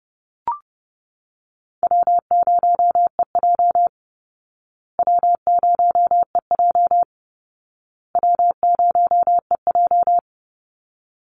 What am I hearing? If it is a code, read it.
Morse code, heard as a keyed tone, about 30 wpm, 700 Hz. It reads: W0EJ W0EJ W0EJ
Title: Easy Call Signs Repeated 3x - 30wpm